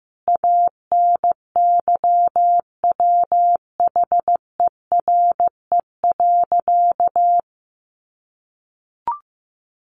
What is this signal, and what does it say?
Morse code, heard as a keyed tone, about 15 wpm, 700 Hz. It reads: ANYWHERE.
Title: Sentences from Top 1400 Words 15wpm